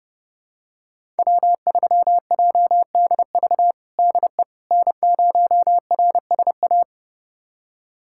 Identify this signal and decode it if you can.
Morse code, heard as a keyed tone, about 32 wpm, 700 Hz. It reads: W3JDV DE N0RSA
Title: USA Callsign DE Callsign - 5 Characters - 30wpm